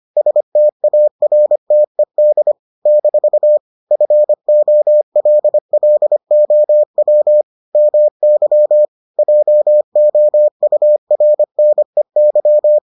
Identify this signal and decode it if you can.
Morse code, heard as a keyed tone, about 25 wpm, 600 Hz. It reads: STARTED - FOLLOW MY JOURNEY